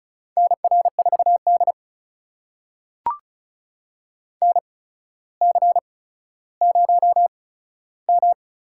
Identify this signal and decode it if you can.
Morse code, heard as a keyed tone, about 34 wpm, spaced as for 15 wpm, 700 Hz. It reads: NR4D NC0M
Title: USA Call Signs - 4 Characters - 8x Character Spacing - 35wpm